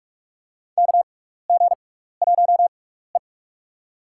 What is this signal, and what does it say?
Morse code, heard as a keyed tone, about 47 wpm, spaced as for 16 wpm, 700 Hz. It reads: KG1E
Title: USA Call Signs - 4 Characters - 6x Character Spacing - 45wpm